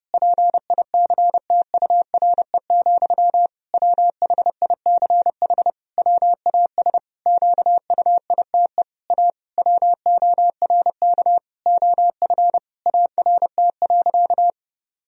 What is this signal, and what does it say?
Morse code, heard as a keyed tone, about 30 wpm, 700 Hz. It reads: PICTURE, WHICH WAS QUITE A WORK OF ART.
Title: Sentences from Top 1000 Words - Review Part 3 - 30wpm